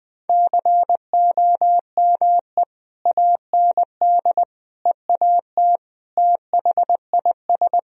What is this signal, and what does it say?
Morse code, heard as a keyed tone, about 20 wpm, 700 Hz. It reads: COME AND EAT THIS
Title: Sentences from Top 1000 Words - Review Part 1 - 20wpm